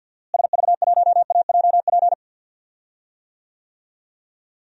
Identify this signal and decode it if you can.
Morse code, heard as a keyed tone, about 52 wpm, 700 Hz. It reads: SV1AJP
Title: DXCC-100 Call Signs - 6 Characters - Repeated 3x - 50wpm